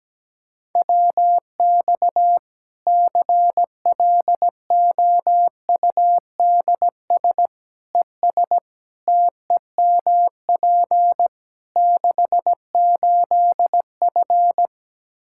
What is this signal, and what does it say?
Morse code, heard as a keyed tone, about 17 wpm, 700 Hz. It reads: WX CLOUDS ES TEMP 68F
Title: Practice QSO - Exchange 2 - Info - 17wpm